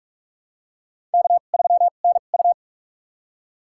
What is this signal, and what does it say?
Morse code, heard as a keyed tone, about 47 wpm, spaced as for 33 wpm, 700 Hz. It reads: K3NU
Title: USA Call Signs - 4 Characters - 2x Character Spacing - 45wpm